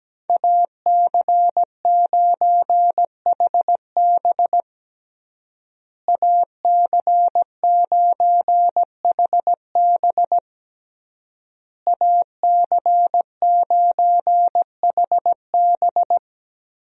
Morse code, heard as a keyed tone, about 17 wpm, 700 Hz. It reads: AC9HB AC9HB AC9HB
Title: USA Call Signs - 5 Characters - Repeated 3x - 17wpm